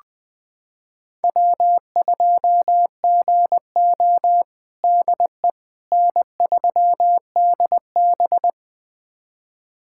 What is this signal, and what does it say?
Morse code, heard as a keyed tone, about 20 wpm, 700 Hz. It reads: W2GO DE N3DB
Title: USA Callsign DE Callsign - 4 Characters - 20wpm